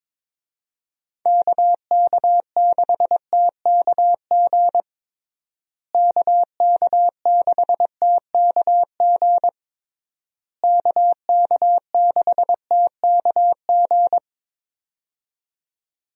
Morse code, heard as a keyed tone, about 22 wpm, 700 Hz. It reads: KK6TKG KK6TKG KK6TKG
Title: USA Call Signs - 6 Characters - Repeated 3x - 22wpm